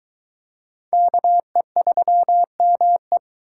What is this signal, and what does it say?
Morse code, heard as a keyed tone, about 22 wpm, 700 Hz. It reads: KE3ME